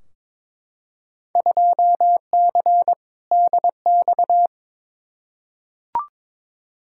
Morse code, heard as a keyed tone, about 22 wpm, 700 Hz. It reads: 2C DX
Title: Field Day Exchanges Encore - 22wpm